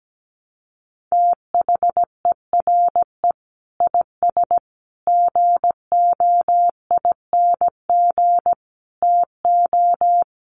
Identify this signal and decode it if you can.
Morse code, heard as a keyed tone, about 17 wpm, 700 Hz. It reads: THERE IS GOING TO
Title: Sets of 5 Words Encore 17wpm